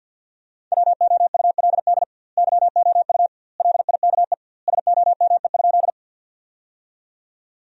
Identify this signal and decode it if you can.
Morse code, heard as a keyed tone, about 50 wpm, 700 Hz. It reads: WOULD YOU LIKE SOME?
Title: 2x Word-Spacing - Sentences from Top 100 Words - 50wpm